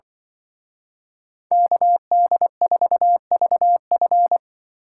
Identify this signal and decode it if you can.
Morse code, heard as a keyed tone, about 23 wpm, 700 Hz. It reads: KD4VF
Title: Speed Racing - USA 2x2 Call Signs 20wpm